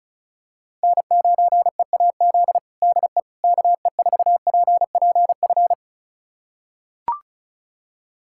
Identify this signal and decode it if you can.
Morse code, heard as a keyed tone, about 36 wpm, 700 Hz. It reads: N9EAZ DE KE4PPF
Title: USA Callsign DE Callsign 35wpm